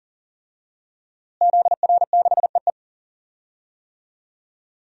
Morse code, heard as a keyed tone, about 40 wpm, 700 Hz. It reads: ZR6EE
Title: DXCC-100 Call Signs - 5 Characters - Repeated 2x - 40wpm